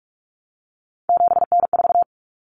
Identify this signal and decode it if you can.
Morse code, heard as a keyed tone, about 44 wpm, 700 Hz. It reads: 7N4
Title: Common Cut Numbers - 3-Digit 45wpm